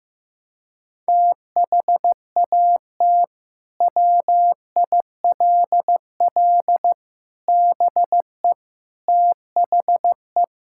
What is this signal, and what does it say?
Morse code, heard as a keyed tone, about 15 wpm, 700 Hz. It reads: THAT WILL BE THE
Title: Sentences from Top 1000 Words - Review Part 6 - 15wpm